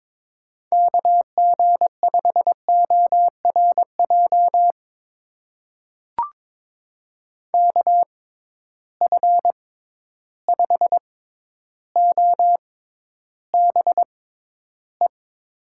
Morse code, heard as a keyed tone, about 21 wpm, spaced as for 11 wpm, 700 Hz. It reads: KG5ORJ KF5OBE
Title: USA Call Signs - 6 Characters - 6x Character Spacing - 22wpm